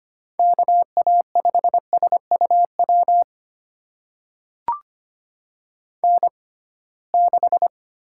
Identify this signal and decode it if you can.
Morse code, heard as a keyed tone, about 25 wpm, spaced as for 19 wpm, 700 Hz. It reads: KA5SUW N6
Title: USA Call Signs - 6x Character Spacing - 25wpm